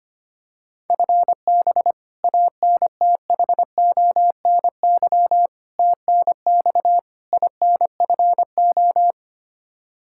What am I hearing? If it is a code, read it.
Morse code, heard as a keyed tone, about 25 wpm, 700 Hz. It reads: FB ANTHONY TNX INFO
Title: Practice QSO - Exchange 3 - More Info - 25wpm